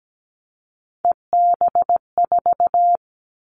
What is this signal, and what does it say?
Morse code, heard as a keyed tone, about 17 wpm, 700 Hz. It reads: EB4
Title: Callsign Prefixes for Top 25 DXCC - Repeated 3x - 17wpm